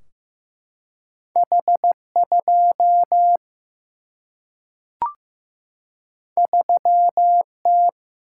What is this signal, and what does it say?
Morse code, heard as a keyed tone, about 15 wpm, 700 Hz. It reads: H2 3T
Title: Single-Letter Single-Number 15wpm